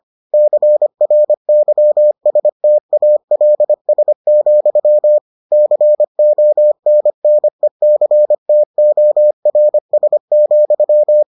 Morse code, heard as a keyed tone, about 25 wpm, 600 Hz. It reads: CRYSTALS, CONNECTORS,